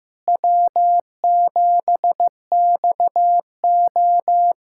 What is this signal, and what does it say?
Morse code, heard as a keyed tone, about 15 wpm, 700 Hz. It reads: W7XO
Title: USA Callsign DE Callsign - 4 Characters - 15wpm